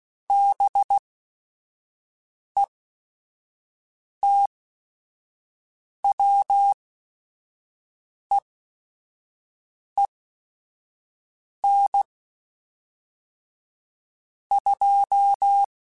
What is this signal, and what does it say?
Morse code, heard as a keyed tone, about 16 wpm, spaced as for 5 wpm, 800 Hz. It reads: BETWEEN 2
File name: Weather Sets / Weather Sets-mp3s / Weather 5 12 @5